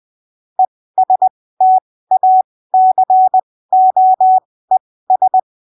Morse code, heard as a keyed tone, about 20 wpm, spaced as for 15 wpm, 750 Hz. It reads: ESTACOES
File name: arquivos / texto / texto 03 20ppm